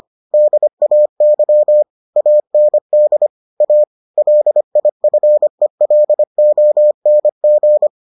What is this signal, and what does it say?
Morse code, heard as a keyed tone, about 25 wpm, 600 Hz. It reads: DAY AND A LIFELONG